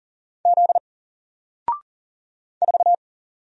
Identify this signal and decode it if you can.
Morse code, heard as a keyed tone, about 40 wpm, 700 Hz. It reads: Z 4
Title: Single Letter-Number 40wpm